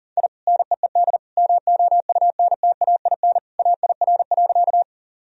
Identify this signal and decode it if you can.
Morse code, heard as a keyed tone, about 40 wpm, 700 Hz. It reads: I NEED MOUNTAIN AIR.
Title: Sentences from Top 1000 Words - Review Part 1 - 40wpm